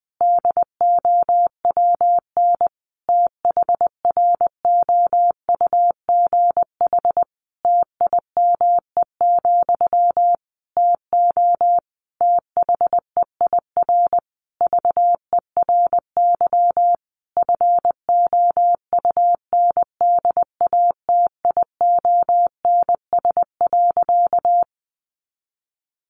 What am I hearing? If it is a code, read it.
Morse code, heard as a keyed tone, about 20 wpm, 700 Hz. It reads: DOWN THROUGH TIME, TO THEIR VERY FOUNDATIONS.
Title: A Sound Of Thunder 20wpm